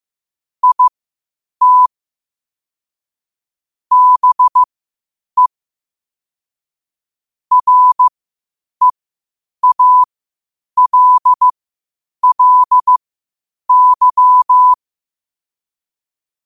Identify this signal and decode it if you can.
Morse code, heard as a keyed tone, about 15 wpm, spaced as for 8 wpm, 1000 Hz. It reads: IT BE REALLY
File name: qotdfiles / 8wpm